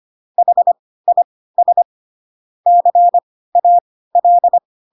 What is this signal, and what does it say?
Morse code, heard as a keyed tone, about 25 wpm, spaced as for 16 wpm, 700 Hz. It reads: HIS CAL
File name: Pinocchio / Pinocchio ch4 10 25 ik1hge